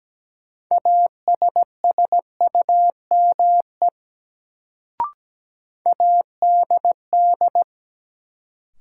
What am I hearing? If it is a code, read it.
Morse code, heard as a keyed tone, about 17 wpm, 700 Hz. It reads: ASSUME ADD
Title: Sentences from Top 1500 Words - New Word Drill Alt - 17wpm